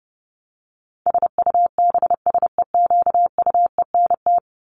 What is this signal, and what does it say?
Morse code, heard as a keyed tone, about 30 wpm, 700 Hz. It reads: SUBSEQUENT
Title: LY Suffix - Top 100 Words with Base Words 30wpm